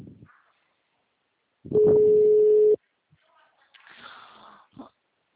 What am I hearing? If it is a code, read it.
Ringback tone (Europe, 425 Hz).
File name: out- -2024- - - .1166